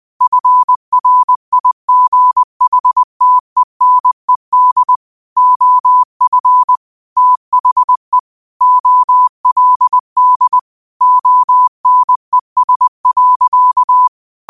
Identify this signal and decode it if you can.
Morse code, heard as a keyed tone, about 20 wpm, 1000 Hz. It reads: FRIGHTENED OF THE OLD ONES.
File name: qotdfiles / 20wpm